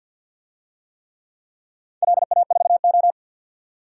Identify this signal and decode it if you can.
Morse code, heard as a keyed tone, about 51 wpm, 700 Hz. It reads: LA4O
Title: DXCC-100 Call Signs - 4 Characters - 50wpm